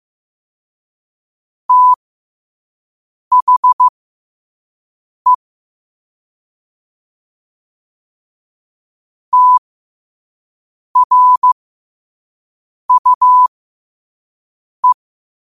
Morse code, heard as a keyed tone, about 15 wpm, spaced as for 5 wpm, 1000 Hz. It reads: THE TRUE
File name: qotdfiles / 5wpm